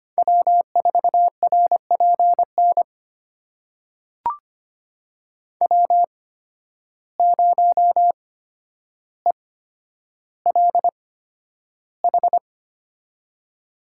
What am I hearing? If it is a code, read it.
Morse code, heard as a keyed tone, about 25 wpm, spaced as for 11 wpm, 700 Hz. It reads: W4RPN W0ELH
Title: USA Call Signs - 5 Characters - 8x Character Spacing - 25wpm